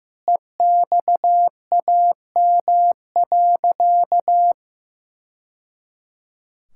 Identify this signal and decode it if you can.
Morse code, heard as a keyed tone, about 15 wpm, 700 Hz. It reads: EXAM.
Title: Sentences from Top 1000 Words - Review Part 5 - 15wpm